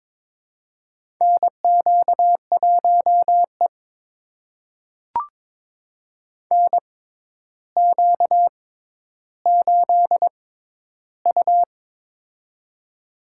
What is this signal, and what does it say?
Morse code, heard as a keyed tone, about 22 wpm, spaced as for 11 wpm, 700 Hz. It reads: NQ1E NQ8U
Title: USA Call Signs - 4 Characters - 6x Character Spacing - 22wpm